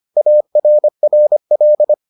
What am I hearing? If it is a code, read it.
Morse code, heard as a keyed tone, about 25 wpm, 600 Hz. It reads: ARRL